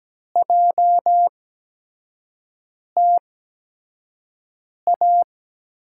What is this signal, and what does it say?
Morse code, heard as a keyed tone, about 17 wpm, spaced as for 5 wpm, 700 Hz. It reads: JTA
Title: USA Call Signs - 5 Characters - 8x Character Spacing - 17wpm